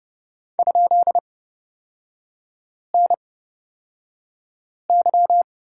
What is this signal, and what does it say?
Morse code, heard as a keyed tone, about 30 wpm, spaced as for 5 wpm, 700 Hz. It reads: ?NY
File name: Lesson 36e - BT - Three Characters 30wpm